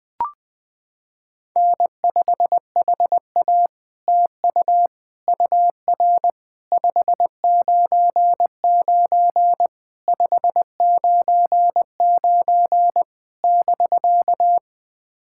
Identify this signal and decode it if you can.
Morse code, heard as a keyed tone, about 20 wpm, 700 Hz. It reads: N5HA TU UR 599 599 <BK>
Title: POTA QSO Practice - 20wpm